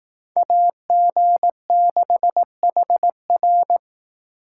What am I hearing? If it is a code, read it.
Morse code, heard as a keyed tone, about 18 wpm, 700 Hz. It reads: AG6HR
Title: Speed Racing - USA 2x2 Call Signs 15wpm